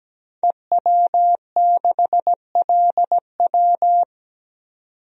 Morse code, heard as a keyed tone, about 17 wpm, 700 Hz. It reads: EW6LW
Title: DXCC-100 Call Signs - 5 Characters - 17wpm